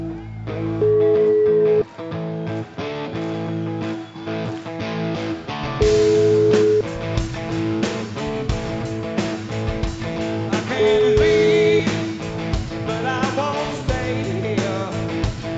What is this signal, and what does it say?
Ringback tone (Europe, 425 Hz), with noise.